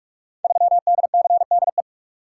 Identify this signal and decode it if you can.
Morse code, heard as a keyed tone, about 44 wpm, 700 Hz. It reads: 3DCDE
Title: Code Groups - 5 Letter:Numbers - 45wpm